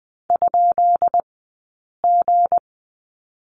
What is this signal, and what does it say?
Morse code, heard as a keyed tone, about 20 wpm, spaced as for 8 wpm, 700 Hz. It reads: ?G
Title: 204 Lesson 33d - Letter j - Two Characters 20wpm